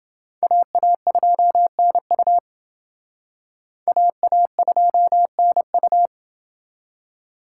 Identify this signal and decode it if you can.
Morse code, heard as a keyed tone, about 29 wpm, 700 Hz. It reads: AA2NU AA2NU